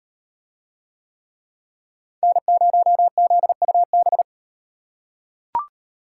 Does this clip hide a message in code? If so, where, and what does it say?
Yes: Morse code from 2.2 to 4.2 s, heard as a keyed tone, about 38 wpm, 700 Hz. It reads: N0ZUB